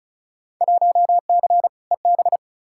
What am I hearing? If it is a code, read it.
Morse code, heard as a keyed tone, about 36 wpm, 700 Hz. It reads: 1C EB
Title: Field Day Exchanges Encore - 35wpm